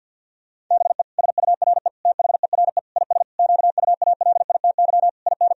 Morse code, heard as a keyed tone, about 50 wpm, 700 Hz. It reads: BE SURE THERE IS QUALITY IN
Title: Sentences from Top 1000 Words - Review Part 4 - 50wpm